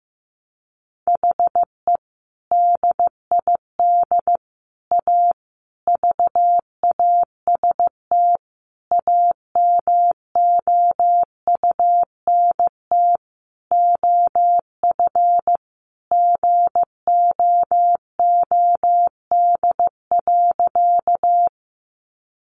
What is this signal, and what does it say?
Morse code, heard as a keyed tone, about 15 wpm, 700 Hz. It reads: HE DID A VAST AMOUNT OF GOOD.
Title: Sentences from Top 1300 Words 15wpm